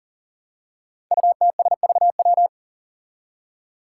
Morse code, heard as a keyed tone, about 40 wpm, 700 Hz. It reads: UTSVW
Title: Five Challenging Letter-Number Combinations 40wpm